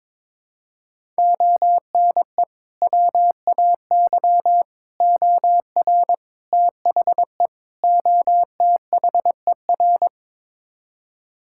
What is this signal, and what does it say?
Morse code, heard as a keyed tone, about 22 wpm, 700 Hz. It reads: ONE WAY OR THE OTHER
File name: practice / 22wpm / Build A Phrase - 5-Word Phrases - 22wpm